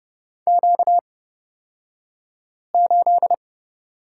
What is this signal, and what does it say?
Morse code, heard as a keyed tone, about 31 wpm, spaced as for 5 wpm, 700 Hz. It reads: Q8
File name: Lesson 38d - Letter q - Two Characters 30wpm